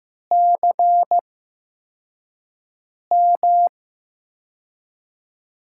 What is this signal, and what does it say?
Morse code, heard as a keyed tone, about 15 wpm, spaced as for 4 wpm, 700 Hz. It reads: CM
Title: USA Call Signs - 6 Characters - 8x Character Spacing - 15wpm